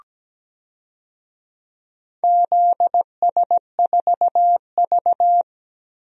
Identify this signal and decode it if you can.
Morse code, heard as a keyed tone, about 17 wpm, 700 Hz. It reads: ZS4V
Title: DXCC-100 Call Signs - 4 Characters - 17wpm